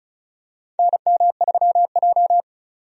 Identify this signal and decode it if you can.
Morse code, heard as a keyed tone, about 36 wpm, 690 Hz. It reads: NM3J
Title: USA Call Signs - 4 Characters - 35wpm